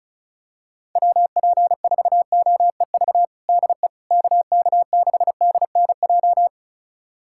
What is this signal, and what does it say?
Morse code, heard as a keyed tone, about 36 wpm, 700 Hz. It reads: WP4OEV DE KK6DNJ
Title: USA Callsign DE Callsign - 6 Characters - 35wpm